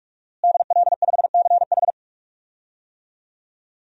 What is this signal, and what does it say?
Morse code, heard as a keyed tone, about 44 wpm, 700 Hz. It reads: DL5CH
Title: DXCC-100 Call Signs - 5 Characters - Repeated 3x - 45wpm